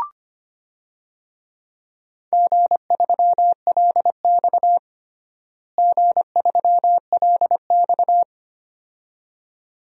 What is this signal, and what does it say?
Morse code, heard as a keyed tone, about 25 wpm, 700 Hz. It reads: G3LX G3LX
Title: DXCC-100 Call Signs - 4 Characters - Repeated 2x - 25wpm